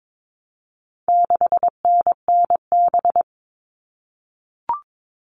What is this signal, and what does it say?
Morse code, heard as a keyed tone, about 22 wpm, 700 Hz. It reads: 6NNB
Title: All Cut Numbers - 4-Digit 22wpm